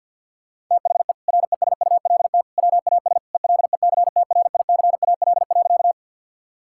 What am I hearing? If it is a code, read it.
Morse code, heard as a keyed tone, about 50 wpm, 700 Hz. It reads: THE RESULT WAS ELECTRICAL.
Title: Sentences from Top 1000 Words - Review Part 6 - 50wpm